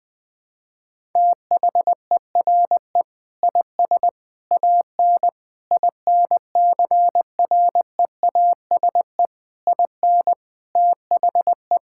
Morse code, heard as a keyed tone, about 20 wpm, 700 Hz. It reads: THERE IS AN INCREASE IN THE
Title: Sentences with 5-Word Phrases - Spoken Beginning - 20wpm